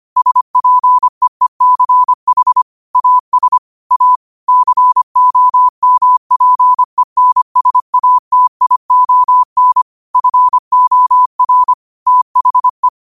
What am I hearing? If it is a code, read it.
Morse code, heard as a keyed tone, about 25 wpm, 1000 Hz. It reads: SPEECH AS A COMPENSATION FOR THE
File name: qotdfiles / 25wpm